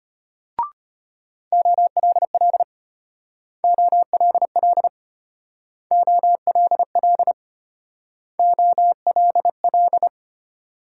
Morse code, heard as a keyed tone, about 31 wpm, 700 Hz. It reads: OLL OLL OLL OLL
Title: Speed-Racing Multi-letter Phonemes - 25wpm